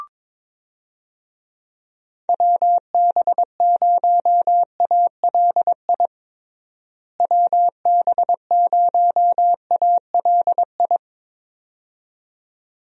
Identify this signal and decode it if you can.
Morse code, heard as a keyed tone, about 22 wpm, 700 Hz. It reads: WB0ALI WB0ALI
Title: USA Call Signs - 6 Characters - Repeated 2x - 22wpm